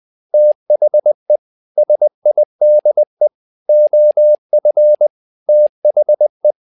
Morse code, heard as a keyed tone, about 20 wpm, 600 Hz. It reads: THE SIDE OF THE
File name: tower trouble 20wpm